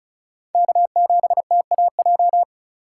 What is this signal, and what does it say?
Morse code, heard as a keyed tone, about 34 wpm, 700 Hz. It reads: K7TAJ
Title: USA Call Signs - 5 Characters - 2x Character Spacing - 35wpm